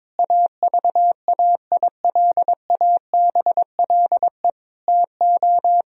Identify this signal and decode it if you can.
Morse code, heard as a keyed tone, about 22 wpm, 700 Hz. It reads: AVAILABLE TO
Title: Sentences from Top 1000 Words - Review Part 3 - 22wpm